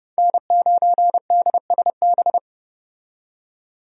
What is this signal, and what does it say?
Morse code, heard as a keyed tone, about 30 wpm, 700 Hz. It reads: N9DSB
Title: USA Call Signs - 5 Characters - 30wpm